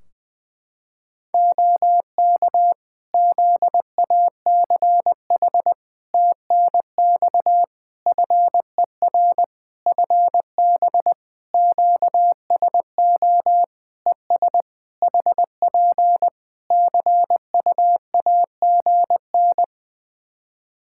Morse code, heard as a keyed tone, about 21 wpm, 700 Hz. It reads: OK ZACH TNX FER FB QSO ES HP CUAGN
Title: K1USN SST QSO Practice 20wpm